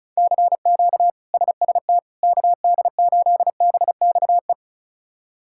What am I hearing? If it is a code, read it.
Morse code, heard as a keyed tone, about 34 wpm, 700 Hz. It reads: CQ SST KD8BXE
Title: K1USN SST QSO Practice - 35wpm